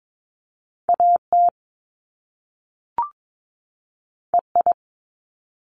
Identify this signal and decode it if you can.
Morse code, heard as a keyed tone, about 22 wpm, 700 Hz. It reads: AT EI
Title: Top Two Letter Combinations in Words 22wpm